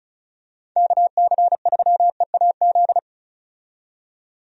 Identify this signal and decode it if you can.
Morse code, heard as a keyed tone, about 34 wpm, 700 Hz. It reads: KC3EAZ
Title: USA Call Signs - 6 Characters - 35wpm